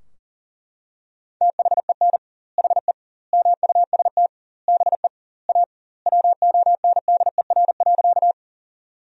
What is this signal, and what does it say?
Morse code, heard as a keyed tone, about 37 wpm, 700 Hz. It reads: THEN HE MUST BE A WONDER.
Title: 2x Word-Spacing - Sentences from Top 500 Words - 40wpm